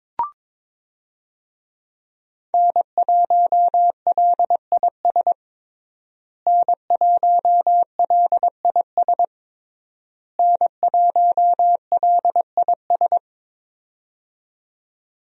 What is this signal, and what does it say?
Morse code, heard as a keyed tone, about 22 wpm, 700 Hz. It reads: N1LIS N1LIS N1LIS
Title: USA Call Signs - 5 Characters - Repeated 3x - 22wpm